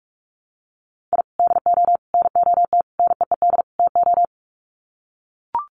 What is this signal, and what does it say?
Morse code, heard as a keyed tone, about 44 wpm, 700 Hz. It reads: I DO NOT NEED TO
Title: Sets of 5 Words 45wpm